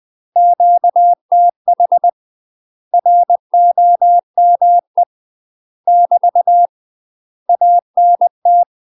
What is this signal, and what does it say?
Morse code, heard as a keyed tone, about 20 wpm, 700 Hz. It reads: QTH ROME <BT> ANT
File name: UK MASTER QSOs TRAINING/20WPM